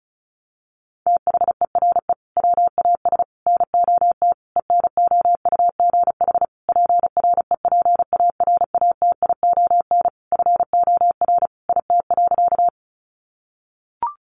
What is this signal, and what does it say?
Morse code, heard as a keyed tone, about 34 wpm, 700 Hz. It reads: THERE WAS NOT ENOUGH PREPARATION FOR IT.
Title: Sentences from Top 1400 Words 35wpm